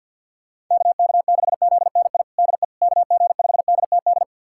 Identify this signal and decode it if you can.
Morse code, heard as a keyed tone, about 46 wpm, 700 Hz. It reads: KK6ZNI DE KG5DTD
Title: USA Callsign DE Callsign - 6 Characters - 50wpm